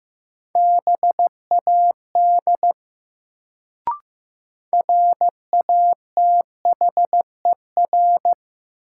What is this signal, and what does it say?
Morse code, heard as a keyed tone, about 15 wpm, 700 Hz. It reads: BAD RATHER
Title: Sentences from Top 700 Words - New Word Drill - 15wpm